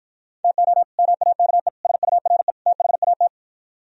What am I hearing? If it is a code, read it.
Morse code, heard as a keyed tone, about 53 wpm, 700 Hz. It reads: TO MAKE SURE THAT